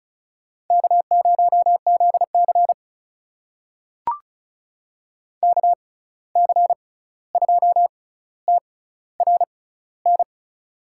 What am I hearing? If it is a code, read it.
Morse code, heard as a keyed tone, about 36 wpm, spaced as for 16 wpm, 700 Hz. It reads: K0ZC KC2TRN
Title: USA Call Signs - 6x Character Spacing - 35wpm